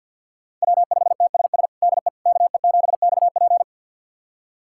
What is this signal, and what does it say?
Morse code, heard as a keyed tone, about 46 wpm, 700 Hz. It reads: W5TSS DE KE7XP
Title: USA Callsign DE Callsign 50wpm